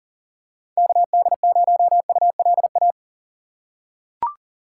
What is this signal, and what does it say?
Morse code, heard as a keyed tone, about 40 wpm, 700 Hz. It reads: KD0ULA
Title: USA Call Signs - 6 Characters - 2x Character Spacing - 40wpm